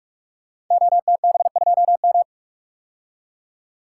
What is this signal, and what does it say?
Morse code, heard as a keyed tone, about 44 wpm, 690 Hz. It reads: OTB2M
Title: Code Groups - 5 Letter:Numbers - 45wpm